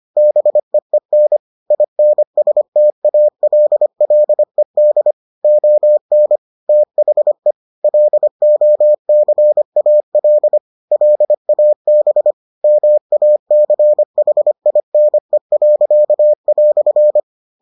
Morse code, heard as a keyed tone, about 25 wpm, 600 Hz. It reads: BEEN INSTALLED ON THE LOCAL LAB MACHINE."